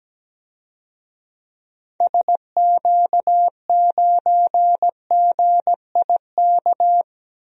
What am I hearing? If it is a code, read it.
Morse code, heard as a keyed tone, about 17 wpm, 700 Hz. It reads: SQ9GIK
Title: DXCC-100 Call Signs - 6 Characters - Repeated 3x - 17wpm